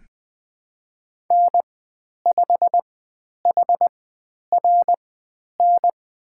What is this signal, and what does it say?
Morse code, heard as a keyed tone, about 20 wpm, spaced as for 10 wpm, 700 Hz. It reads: N5HRN
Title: 102 Lesson 17g - Letter u - Call Signs 20wpm